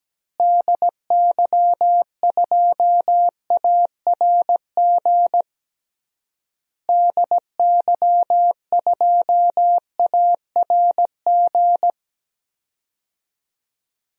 Morse code, heard as a keyed tone, about 17 wpm, 700 Hz. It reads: DY2ARG DY2ARG
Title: DXCC-100 Call Signs - 6 Characters - Repeated 3x - 17wpm